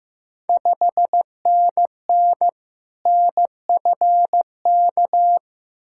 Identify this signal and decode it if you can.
Morse code, heard as a keyed tone, about 15 wpm, 700 Hz. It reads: 5NN NFK
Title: Ontario QSO Party Exchange 15wpm